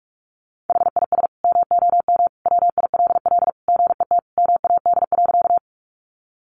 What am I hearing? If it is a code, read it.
Morse code, heard as a keyed tone, about 44 wpm, 700 Hz. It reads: HIS MOM WILL GET MAD.
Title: Additional Easy Phrases 45wpm